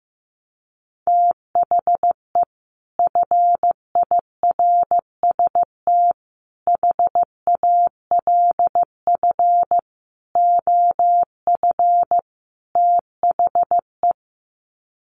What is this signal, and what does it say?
Morse code, heard as a keyed tone, about 15 wpm, 700 Hz. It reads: THE FIRST HALF OF THE
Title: Sets of 5 Words 15wpm